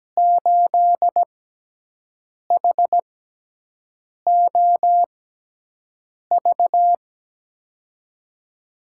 Morse code, heard as a keyed tone, about 17 wpm, spaced as for 6 wpm, 700 Hz. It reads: 8HOV
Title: USA Call Signs - 6x Character Spacing - 17wpm